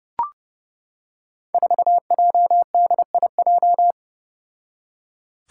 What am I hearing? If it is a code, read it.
Morse code, heard as a keyed tone, about 30 wpm, 700 Hz. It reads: 4JDIJ
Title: Five Challenging Letter-Number Combinations 30wpm